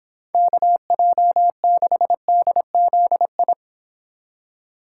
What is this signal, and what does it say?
Morse code, heard as a keyed tone, about 26 wpm, 700 Hz. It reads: KJ6DZI